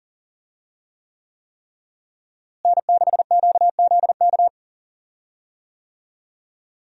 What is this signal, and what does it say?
Morse code, heard as a keyed tone, about 40 wpm, 700 Hz. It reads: N6QZK
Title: USA Call Signs - 5 Characters - Repeated 3x - 40wpm